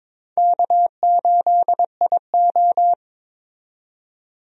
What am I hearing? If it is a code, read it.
Morse code, heard as a keyed tone, about 22 wpm, 700 Hz. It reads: K8IO
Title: USA Call Signs - 4 Characters - 22wpm